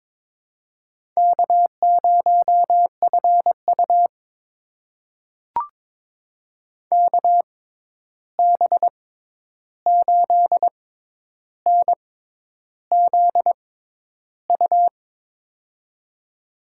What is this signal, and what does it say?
Morse code, heard as a keyed tone, about 22 wpm, spaced as for 10 wpm, 700 Hz. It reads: K0FU KB8NZU
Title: USA Call Signs - 6x Character Spacing - 22wpm